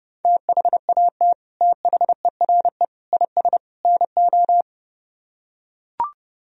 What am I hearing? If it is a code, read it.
Morse code, heard as a keyed tone, about 29 wpm, 700 Hz. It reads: THAT THERE IS NO